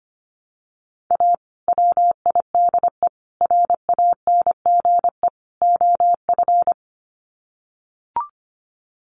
Morse code, heard as a keyed tone, about 25 wpm, 700 Hz. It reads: A WIDE RANGE OF
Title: Sets of 4 Words 25wpm